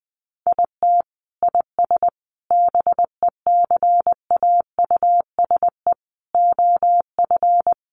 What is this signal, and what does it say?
Morse code, heard as a keyed tone, about 20 wpm, 700 Hz. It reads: IT IS BECAUSE OF
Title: Sentences from Top 600 Words 20wpm